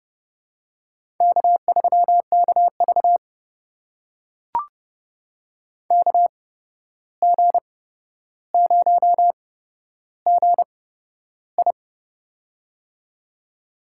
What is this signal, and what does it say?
Morse code, heard as a keyed tone, about 30 wpm, spaced as for 12 wpm, 700 Hz. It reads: K3KV KG0GI
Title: USA Call Signs - 8x Character Spacing - 30wpm